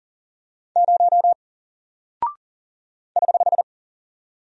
Morse code, heard as a keyed tone, about 40 wpm, 700 Hz. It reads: 0 <HH>
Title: Single Character (Letter:Number:Punctuation:Prosign) Alt 40wpm